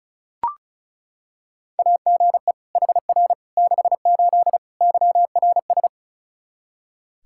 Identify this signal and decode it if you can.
Morse code, heard as a keyed tone, about 34 wpm, 700 Hz. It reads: AGE HR 68 YRS
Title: K1USN SST QSO Practice 35wpm